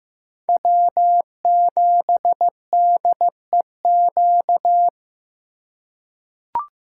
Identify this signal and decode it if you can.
Morse code, heard as a keyed tone, about 15 wpm, 700 Hz. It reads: W7DEQ
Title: USA Call Signs - 5 Characters - 2x Character Spacing - 15wpm